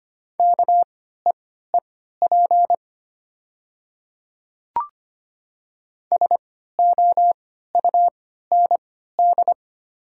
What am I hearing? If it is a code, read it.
Morse code, heard as a keyed tone, about 25 wpm, spaced as for 14 wpm, 700 Hz. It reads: KEEP SOUND
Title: 3x Character Spacing - Top 300 Words - 25wpm